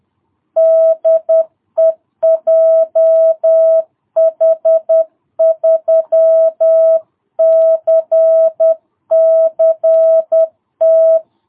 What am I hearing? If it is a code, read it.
Morse code, heard as a keyed tone, about 10 wpm, 660 Hz. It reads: DE JH3CCT
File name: jh8cbh / 2022cw